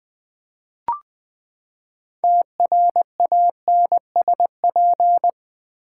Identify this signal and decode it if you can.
Morse code, heard as a keyed tone, about 20 wpm, 700 Hz. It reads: TRANSP
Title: Build A Word - 6 Letters At A Time - 12-Letter Words - 20wpm